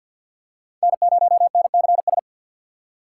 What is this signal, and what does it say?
Morse code, heard as a keyed tone, about 47 wpm, 700 Hz. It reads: N0NCS
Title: USA Call Signs - 5 Characters - 4x Character Spacing - 50wpm